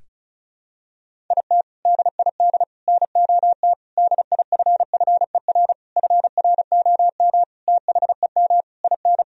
Morse code, heard as a keyed tone, about 34 wpm, 700 Hz. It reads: IT DID NOT DIFFER FROM THEM IN
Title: Sentences from Top 1000 Words - Review Part 1 - 35wpm